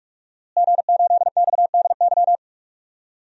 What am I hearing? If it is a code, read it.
Morse code, heard as a keyed tone, about 46 wpm, 700 Hz. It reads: G8XDY
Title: DXCC-100 Call Signs - 5 Characters - Repeated 3x - 45wpm